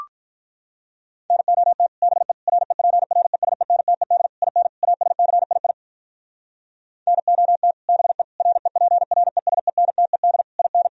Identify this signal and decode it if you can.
Morse code, heard as a keyed tone, about 43 wpm, 700 Hz. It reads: NOT BE REPRESENTED IN ASCII NOT BE REPRESENTED IN
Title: Speed-Racing - Sets of 5 Words 35wpm